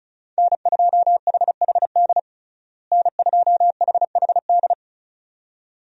Morse code, heard as a keyed tone, about 36 wpm, 700 Hz. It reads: N2HHD N2HHD
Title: USA Call Signs - 5 Characters - Repeated 2x - 35wpm